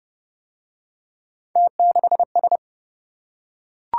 Morse code, heard as a keyed tone, about 26 wpm, 700 Hz. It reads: T6S